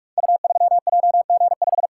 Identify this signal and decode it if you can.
Morse code, heard as a keyed tone, about 44 wpm, 700 Hz. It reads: U3JG5
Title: Code Groups - 5 Letter:Numbers - 45wpm